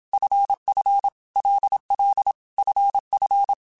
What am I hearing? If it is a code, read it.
Morse code, heard as a keyed tone, about 27 wpm, 800 Hz. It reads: FF LL FF
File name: Basic Letter Groups By7Groups / LFCYQ / LFCYQ@25